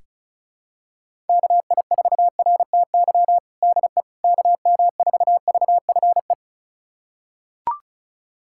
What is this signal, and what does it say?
Morse code, heard as a keyed tone, about 34 wpm, 700 Hz. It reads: KI4RTY DE KM4VFE
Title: USA Callsign DE Callsign - 6 Characters - 35wpm